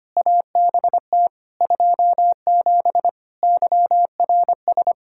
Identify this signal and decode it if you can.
Morse code, heard as a keyed tone, about 25 wpm, 700 Hz. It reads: ABT 27 YRS
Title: Practice QSO - Exchange 3 - More Info - 25wpm